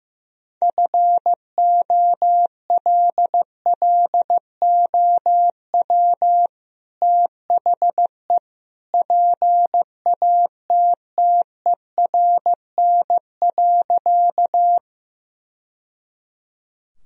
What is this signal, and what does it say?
Morse code, heard as a keyed tone, about 15 wpm, 700 Hz. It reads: FOLLOW THE PATTERN.
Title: Sentences from Top 1000 Words - Review Part 2 - 15wpm